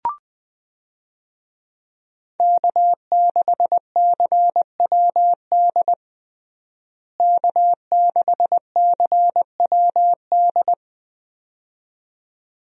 Morse code, heard as a keyed tone, about 20 wpm, 700 Hz. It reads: K6CWD K6CWD
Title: USA Call Signs - 5 Characters - Repeated 2x - 20wpm